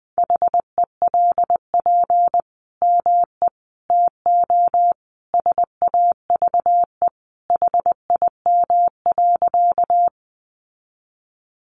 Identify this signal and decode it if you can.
Morse code, heard as a keyed tone, about 20 wpm, 700 Hz. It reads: HELP ME TO SAVE HIM.
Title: Sentences from Top 1100 Words 20wpm